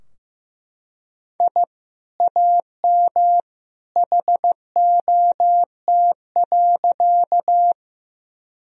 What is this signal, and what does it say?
Morse code, heard as a keyed tone, about 15 wpm, 700 Hz. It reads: I AM HOT.
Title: Sentences from Top 1000 Words - Review Part 3 - 15wpm